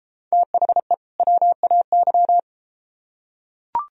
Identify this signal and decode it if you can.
Morse code, heard as a keyed tone, about 32 wpm, 700 Hz. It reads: THE WAY